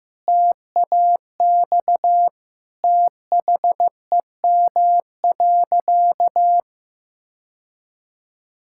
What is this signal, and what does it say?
Morse code, heard as a keyed tone, about 15 wpm, 700 Hz. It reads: TAX THEM.
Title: Sentences from Top 1000 Words - Review Part 5 - 15wpm